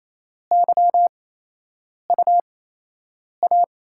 Morse code, heard as a keyed tone, about 28 wpm, spaced as for 8 wpm, 700 Hz. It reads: YUA
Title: USA Call Signs - 8x Character Spacing - 28wpm